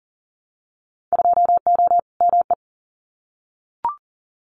Morse code, heard as a keyed tone, about 40 wpm, 700 Hz. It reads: 2O ME
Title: Winter Field Day Exchanges Encore 40wpm